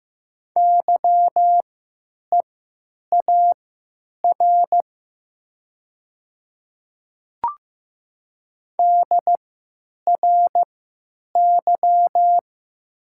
Morse code, heard as a keyed tone, about 15 wpm, spaced as for 9 wpm, 700 Hz. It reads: YEAR DRY
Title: 3x Character Spacing - Top 500 Words - 15wpm